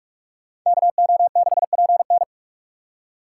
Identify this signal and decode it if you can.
Morse code, heard as a keyed tone, about 44 wpm, 700 Hz. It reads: KO6PN